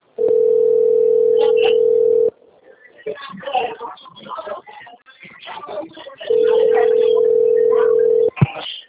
Ringback tone (North America), 440 + 480 Hz.
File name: rg-1600-unknown- - - .30182